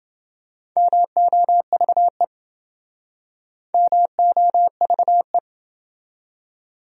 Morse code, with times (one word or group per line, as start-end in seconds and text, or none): 0.8-2.2 MOVE
3.7-5.4 MOVE